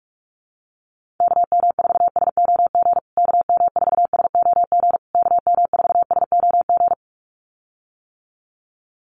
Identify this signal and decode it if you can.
Morse code, heard as a keyed tone, about 44 wpm, 700 Hz. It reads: KM4SOG KM4SOG KM4SOG
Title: USA Call Signs Repeated 3x 45wpm